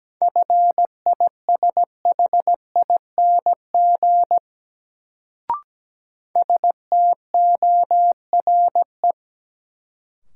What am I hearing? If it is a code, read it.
Morse code, heard as a keyed tone, about 17 wpm, 700 Hz. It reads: FISHING STORE
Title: Sentences from Top 1000 Words - Review Part 5 - New Word Drill - 17wpm